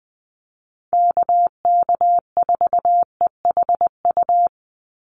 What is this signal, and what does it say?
Morse code, heard as a keyed tone, about 20 wpm, 700 Hz. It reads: KK4EHU
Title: USA Call Signs 20wpm